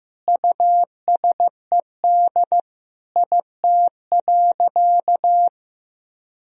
Morse code, heard as a keyed tone, about 15 wpm, 700 Hz. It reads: USED IT.
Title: Sentences from Top 1000 Words - Review Part 2 - 15wpm